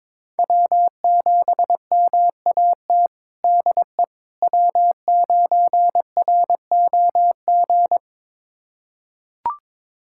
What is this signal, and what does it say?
Morse code, heard as a keyed tone, about 22 wpm, 700 Hz. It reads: W7MAT DE W9ROG
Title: USA Callsign DE Callsign - 5 Characters - 22wpm